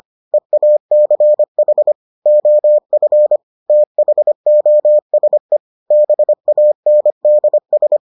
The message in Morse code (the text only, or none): EACH OF THOSE BANDS